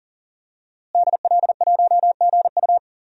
Morse code, heard as a keyed tone, about 40 wpm, 700 Hz. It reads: DL1GU
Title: DXCC-100 Call Signs - 5 Characters - Repeated 2x - 40wpm